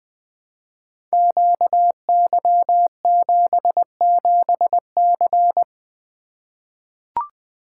Morse code, heard as a keyed tone, about 21 wpm, 700 Hz. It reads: QY77C
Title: Code Groups - 5 Letter:Numbers - 20wpm